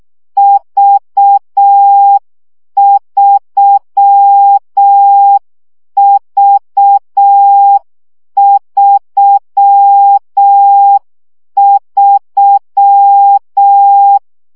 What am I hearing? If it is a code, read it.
Morse code, heard as a keyed tone, about 6 wpm, 800 Hz. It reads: V3V33
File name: CW gyakorlas v 3 5 baud